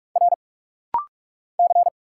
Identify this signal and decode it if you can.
Morse code, heard as a keyed tone, about 46 wpm, 700 Hz. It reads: R C
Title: Single Letter-Number Alt - Rapid-Fire 45wpm